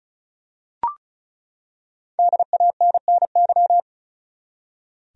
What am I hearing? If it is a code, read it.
Morse code, heard as a keyed tone, about 34 wpm, 700 Hz. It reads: DANNY
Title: USA Common Names - 35wpm